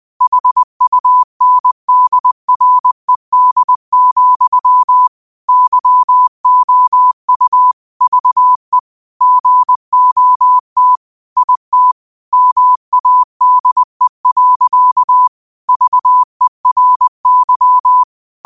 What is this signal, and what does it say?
Morse code, heard as a keyed tone, about 20 wpm, 1000 Hz. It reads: HUNDRED, YOU VE GOT IT MADE. VERY